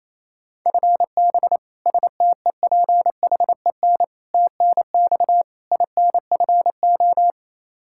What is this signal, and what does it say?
Morse code, heard as a keyed tone, about 28 wpm, 700 Hz. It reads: FB STEPHEN TNX INFO
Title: Practice QSO - Exchange 3 - More Info - 28wpm